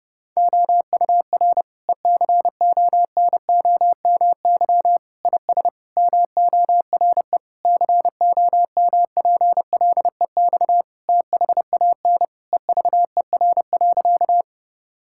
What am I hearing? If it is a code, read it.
Morse code, heard as a keyed tone, about 30 wpm, 700 Hz. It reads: OUR ECONOMY IS MORE COMPLEX THAN EVER.
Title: Sentences from Top 1000 Words - Review Part 6 - 30wpm